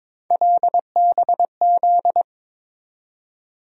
Morse code, heard as a keyed tone, about 22 wpm, 700 Hz. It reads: LBZ
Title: USA Call Sign Postfixes - 3 Letters 22wpm